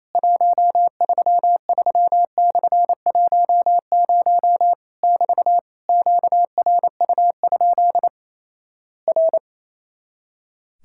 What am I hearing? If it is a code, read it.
Morse code, heard as a keyed tone, about 28 wpm, 700 Hz. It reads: 133/10 <BT> QRU? R
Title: Worked All Europe DX Contest - QTC Exchange - 28wpm